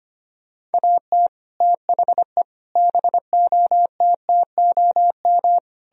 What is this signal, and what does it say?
Morse code, heard as a keyed tone, about 32 wpm, 700 Hz. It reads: AT THE BOTTOM